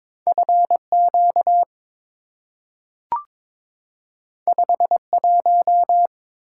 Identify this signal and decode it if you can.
Morse code, heard as a keyed tone, about 22 wpm, 700 Hz. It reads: FQ 51
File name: practice / 22wpm / Two Letter-Number Combinations - 22wpm